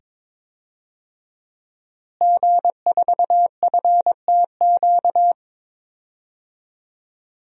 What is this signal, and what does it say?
Morse code, heard as a keyed tone, about 22 wpm, 700 Hz. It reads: G4FTQ
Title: DXCC-100 Call Signs - 5 Characters - Repeated 2x - 22wpm